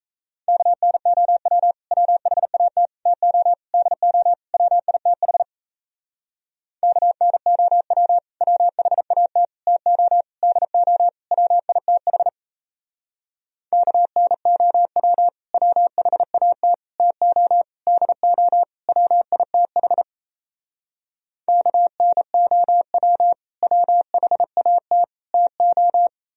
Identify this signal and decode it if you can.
Morse code, heard as a keyed tone, about 36 wpm, 700 Hz. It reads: KNOW WHAT TO DO WITH KNOW WHAT TO DO WITH KNOW WHAT TO DO WITH KNOW WHAT TO